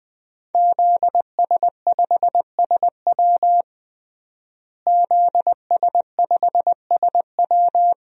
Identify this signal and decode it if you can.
Morse code, heard as a keyed tone, about 20 wpm, 700 Hz. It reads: ZS5SW ZS5SW
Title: DXCC-100 Call Signs - 5 Characters - Repeated 2x - 20wpm